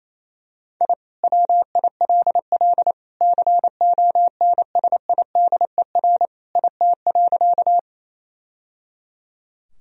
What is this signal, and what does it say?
Morse code, heard as a keyed tone, about 27 wpm, 700 Hz. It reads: I WILL CONSIDER IT.
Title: Sentences from Top 1000 Words - Review Part 4 - 28wpm